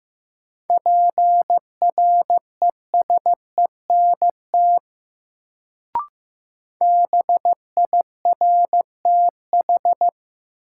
Morse code, heard as a keyed tone, about 15 wpm, 700 Hz. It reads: PRESENT BIRTH
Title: Sentences from Top 800 Words - New Word Drill - 15wpm